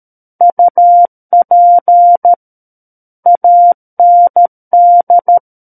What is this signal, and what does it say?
Morse code, heard as a keyed tone, about 13 wpm, 700 Hz. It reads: UP AND
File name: Book One / 04-Chapter 4 - 13 WPM